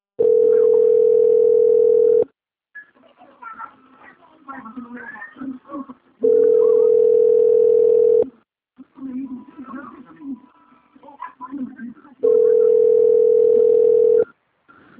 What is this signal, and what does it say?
Ringback tone (North America), 440 + 480 Hz, with speech.
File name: out- - - - - .18764